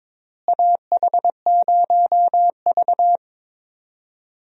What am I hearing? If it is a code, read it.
Morse code, heard as a keyed tone, about 22 wpm, 700 Hz. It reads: AH0V
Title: USA Call Signs - 4 Characters - Repeated 3x - 22wpm